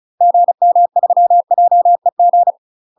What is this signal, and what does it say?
Morse code, heard as a keyed tone, about 35 wpm, 700 Hz. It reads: GM3JEG
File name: UK MASTER QSOs TRAINING / 35WPM / UK MASTER QSOs 0036